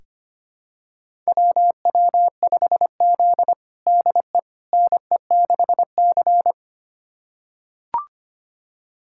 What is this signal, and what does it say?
Morse code, heard as a keyed tone, about 25 wpm, 700 Hz. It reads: WW5Z DE NE6C
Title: USA Callsign DE Callsign - 4 Characters - 25wpm